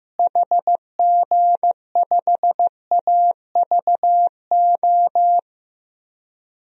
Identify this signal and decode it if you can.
Morse code, heard as a keyed tone, about 15 wpm, 700 Hz. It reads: HG5AVO
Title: DXCC-100 Call Signs - 6 Characters - Repeated 2x - 15wpm